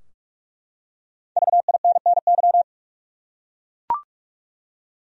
Morse code, heard as a keyed tone, about 44 wpm, 700 Hz. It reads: VINNY
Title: USA Common Names - Max 5 Letters - 45wpm